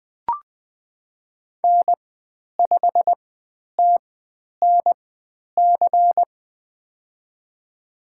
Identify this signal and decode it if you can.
Morse code, heard as a keyed tone, about 20 wpm, spaced as for 10 wpm, 700 Hz. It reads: N5TNC
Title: 095 Lesson 16g - Letter c - Call Signs 20wpm